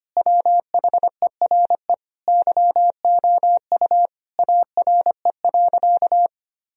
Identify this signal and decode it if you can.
Morse code, heard as a keyed tone, about 25 wpm, 700 Hz. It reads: WHERE YOU ARE.
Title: Sentences from Top 1000 Words - Review Part 1 - 25wpm